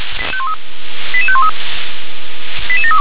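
Five-tone selective call (ZVEI), heard as quick runs of five tones, tones of about 70 ms. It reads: 88513 88732
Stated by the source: ZVEI five-tone call received by SDR.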